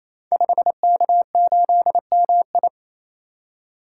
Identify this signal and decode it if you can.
Morse code, heard as a keyed tone, about 27 wpm, 700 Hz. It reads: 5K8MI
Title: Code Groups - 5 Letter:Numbers - 28wpm